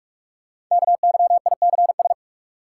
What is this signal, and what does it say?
Morse code, heard as a keyed tone, about 44 wpm, 690 Hz. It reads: KYICS
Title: Code Groups - 5 Letters - 45wpm